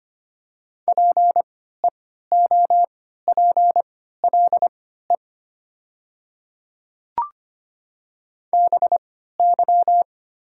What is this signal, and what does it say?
Morse code, heard as a keyed tone, about 25 wpm, spaced as for 14 wpm, 700 Hz. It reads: PEOPLE BY
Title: 3x Character Spacing - Top 100 Words - 25wpm